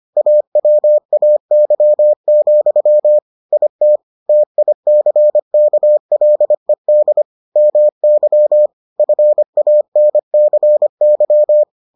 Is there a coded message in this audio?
Yes: Morse code from 0.2 to 11.6 s, heard as a keyed tone, about 25 wpm, 600 Hz. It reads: AWAY, IT TICKLED MY FANCY